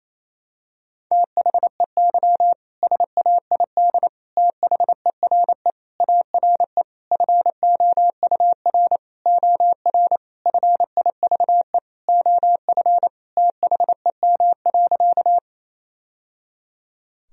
Morse code, heard as a keyed tone, about 28 wpm, 700 Hz. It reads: THEY SAID THERE ARE FOUR OR FIVE OF THEM.
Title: Sentences from Top 1000 Words - Review Part 2 - 28wpm